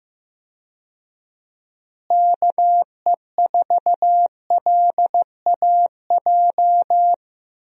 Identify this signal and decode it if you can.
Morse code, heard as a keyed tone, about 15 wpm, 700 Hz. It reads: KE4LAJ
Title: USA Call Signs - 6 Characters - Repeated 3x - 15wpm